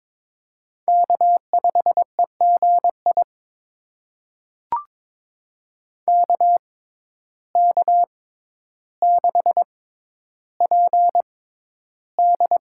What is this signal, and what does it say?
Morse code, heard as a keyed tone, about 22 wpm, spaced as for 11 wpm, 700 Hz. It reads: K5EGI KK6PD
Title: USA Call Signs - 5 Characters - 6x Character Spacing - 22wpm